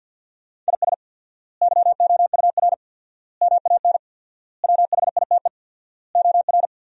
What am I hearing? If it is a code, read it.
Morse code, heard as a keyed tone, about 47 wpm, 700 Hz. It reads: IS YOUR MAN WHITE OR
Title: 4x Word-Spacing - Sentences from Top 400 Words - 50wpm